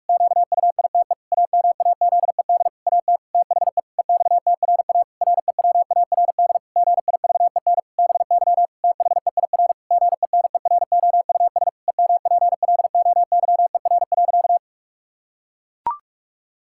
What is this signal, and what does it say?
Morse code, heard as a keyed tone, about 41 wpm, 700 Hz. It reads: QUITE AMAZED AT THE EXTRA REWARD GIVEN BY THEIR GENEROUS EMPLOYER.
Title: Pangram Challenge v2 - 45wpm